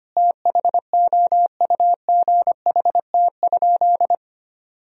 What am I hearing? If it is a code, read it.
Morse code, heard as a keyed tone, about 25 wpm, 700 Hz. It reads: THOUGHT?
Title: Sentences from Top 300 Words and Abbreviations 25wpm